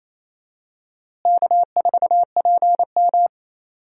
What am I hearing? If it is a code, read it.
Morse code, heard as a keyed tone, about 28 wpm, 700 Hz. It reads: K4PM
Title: USA Call Signs - 4 Characters - Repeated 3x - 28wpm